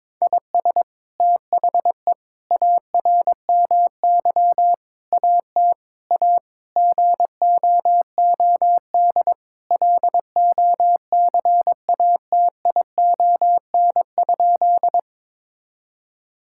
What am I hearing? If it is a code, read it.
Morse code, heard as a keyed tone, about 22 wpm, 700 Hz. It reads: IS THE ARMY AT A GOOD LOCATION?
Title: Sentences from Top 1000 Words - Review Part 6 - 22wpm